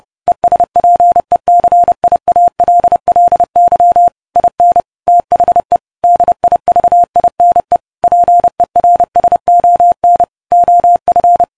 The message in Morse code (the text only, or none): ESPECIALLY IN THE DIVINE PERSON OF